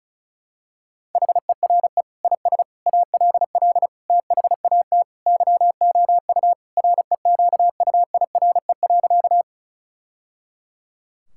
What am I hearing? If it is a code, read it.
Morse code, heard as a keyed tone, about 36 wpm, 700 Hz. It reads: HERE IS ALL THAT YOU REQUIRE.
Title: Sentences from Top 1000 Words - Review Part 6 - 35wpm